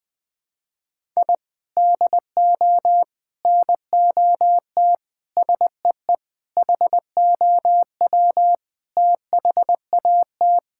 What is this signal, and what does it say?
Morse code, heard as a keyed tone, about 20 wpm, 700 Hz. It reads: I DO NOT SEE HOW THAT
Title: Sentences with 5-Word Phrases - 20wpm